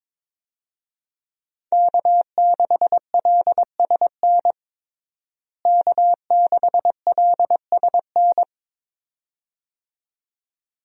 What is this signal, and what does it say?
Morse code, heard as a keyed tone, about 22 wpm, 700 Hz. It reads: K6LSN K6LSN
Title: USA Call Signs - 5 Characters - Repeated 2x - 22wpm